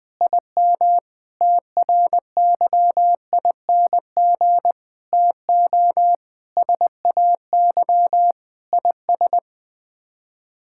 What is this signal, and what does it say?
Morse code, heard as a keyed tone, about 21 wpm, 700 Hz. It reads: IM TRYING TO SAY IS
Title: Sets of 8 Words - 20wpm